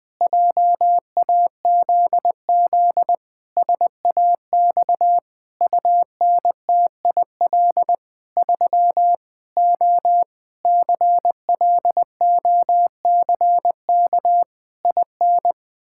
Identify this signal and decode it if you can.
Morse code, heard as a keyed tone, about 20 wpm, 700 Hz. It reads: JAZZ SAX UNTIL 3 O CLOCK IN
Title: Pangram Challenge v2 - 20wpm